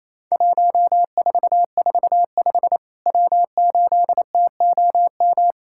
Morse code, heard as a keyed tone, about 27 wpm, 700 Hz. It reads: 1445 W8TOM
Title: Worked All Europe DX Contest - QTC Exchange - 28wpm